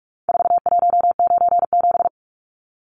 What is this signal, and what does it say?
Morse code, heard as a keyed tone, about 44 wpm, 700 Hz. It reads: 4197
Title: Numbers - 4-Digit 45wpm